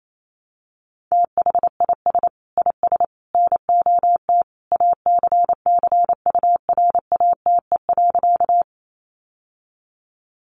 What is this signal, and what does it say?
Morse code, heard as a keyed tone, about 28 wpm, 700 Hz. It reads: THIS IS NOT ACCURATE.
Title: Sentences from Top 1200 Words 28wpm